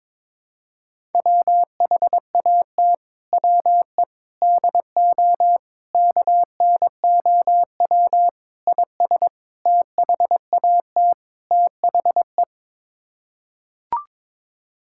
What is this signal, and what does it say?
Morse code, heard as a keyed tone, about 22 wpm, 700 Hz. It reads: WHAT WE DO KNOW IS THAT THE
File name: practice / 22wpm / Sets of 7 Words - 22wpm